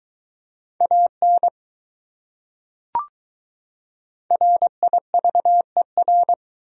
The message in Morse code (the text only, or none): AN RIVER